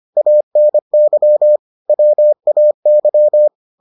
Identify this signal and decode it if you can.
Morse code, heard as a keyed tone, about 25 wpm, 600 Hz. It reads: ANY WAY